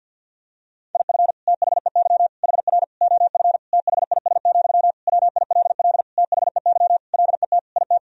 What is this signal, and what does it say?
Morse code, heard as a keyed tone, about 52 wpm, 700 Hz. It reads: IF THEY HR OF THIS, WILL THEY LET IT
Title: Sentences from Top 400 Words and Abbreviations 50wpm